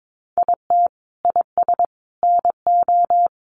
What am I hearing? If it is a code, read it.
Morse code, heard as a keyed tone, about 22 wpm, 700 Hz. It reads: IT IS NO
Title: Sentences from Top 500 Words 22wpm